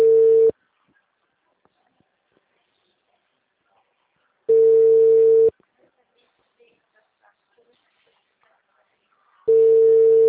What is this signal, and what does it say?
Ringback tone (China, 450 Hz), with speech.